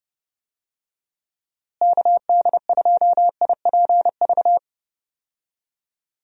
Morse code, heard as a keyed tone, about 30 wpm, 700 Hz. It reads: KD2IPV
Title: USA Call Signs - 6 Characters - Repeated 3x - 30wpm